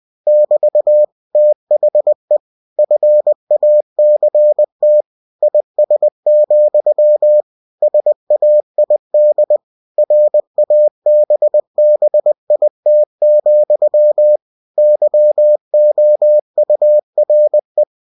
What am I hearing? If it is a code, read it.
Morse code, heard as a keyed tone, about 20 wpm, 600 Hz. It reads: <BT> THE FACT IS, SAID RABBIT, YOURE
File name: cw-books / winnie-the-pooh / winnie-20wpm-page-0017